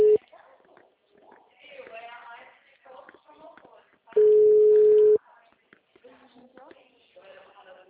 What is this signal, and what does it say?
Ringback tone (Europe, 425 Hz), with speech.